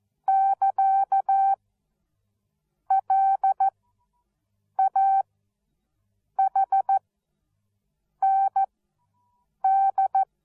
Morse code, heard as a keyed tone, about 14 wpm, spaced as for 6 wpm, 780 Hz. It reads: <KA> LAHND